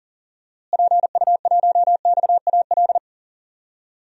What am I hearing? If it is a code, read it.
Morse code, heard as a keyed tone, about 40 wpm, 700 Hz. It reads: PU1XAL
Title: DXCC-100 Call Signs - 6 Characters - Repeated 3x - 40wpm